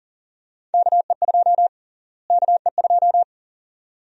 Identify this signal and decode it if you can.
Morse code, heard as a keyed tone, about 40 wpm, 700 Hz. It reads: KE2 KE2
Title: USA Call Sign Prefixes - Repeated 2x - 40wpm